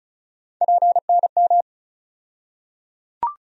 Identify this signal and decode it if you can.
Morse code, heard as a keyed tone, about 36 wpm, 700 Hz. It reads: PNM
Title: USA Call Sign Postfixes - 3 Letters 35wpm